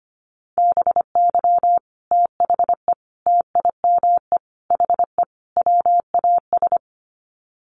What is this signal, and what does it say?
Morse code, heard as a keyed tone, about 25 wpm, 700 Hz. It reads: BY THE TIME HE WAS
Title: Sets of 5 Words Encore 25wpm